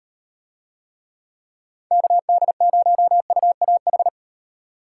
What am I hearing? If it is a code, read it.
Morse code, heard as a keyed tone, about 38 wpm, 700 Hz. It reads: KD0UAH